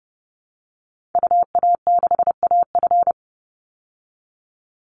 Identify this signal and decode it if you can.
Morse code, heard as a keyed tone, about 30 wpm, 700 Hz. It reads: UA6AF
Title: Call Signs Repeated 3x - 30wpm